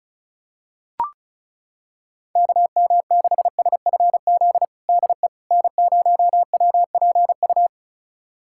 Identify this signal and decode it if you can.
Morse code, heard as a keyed tone, about 36 wpm, 690 Hz. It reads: KM6SFZ DE N0WPU
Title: USA Callsign DE Callsign 35wpm